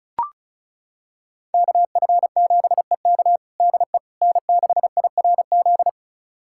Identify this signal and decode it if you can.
Morse code, heard as a keyed tone, about 34 wpm, 700 Hz. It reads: KF7EK DE N6IRZ
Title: USA Callsign DE Callsign - 5 Characters - 35wpm